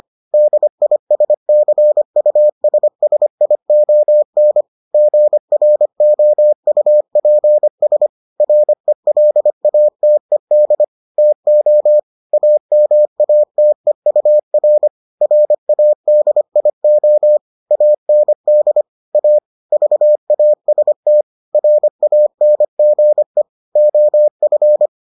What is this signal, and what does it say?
Morse code, heard as a keyed tone, about 25 wpm, 600 Hz. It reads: DISCUSSION GROUPS RELATED TO AMATEUR RADIO AND A VAST RANGE OF